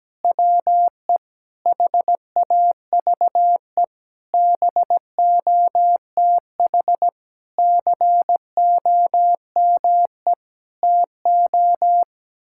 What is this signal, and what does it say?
Morse code, heard as a keyed tone, about 17 wpm, 700 Hz. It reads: WE HAVE BOTH COME TO
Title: Sentences from Top 1600 Words 17wpm